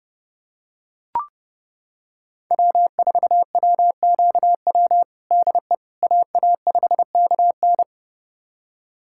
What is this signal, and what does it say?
Morse code, heard as a keyed tone, about 30 wpm, 700 Hz. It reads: W4WQW DE AA5KN
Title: USA Callsign DE Callsign - 5 Characters - 30wpm